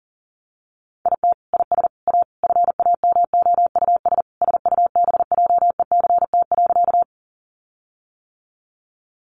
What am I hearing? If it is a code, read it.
Morse code, heard as a keyed tone, about 40 wpm, 700 Hz. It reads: IT IS A FAMOUS SUBJECT.
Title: Sentences from Top 1500 Words 40wpm